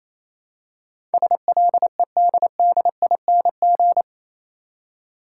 Morse code, heard as a keyed tone, about 27 wpm, 700 Hz. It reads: SLEDDING
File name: Easy Words - Speed-Racing with Hints - 22wpm